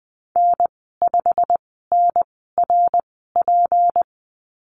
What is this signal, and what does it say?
Morse code, heard as a keyed tone, about 20 wpm, spaced as for 14 wpm, 700 Hz. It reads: N5NRP
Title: 161 Lesson 26g - Letter g - Call Signs 20wpm